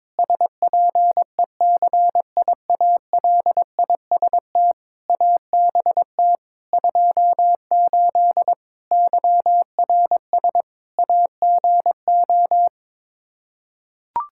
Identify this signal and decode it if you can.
Morse code, heard as a keyed tone, about 22 wpm, 700 Hz. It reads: SPECIALIST ABT 28 YRS AGO
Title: Practice QSO - Exchange 3 - More Info - 22wpm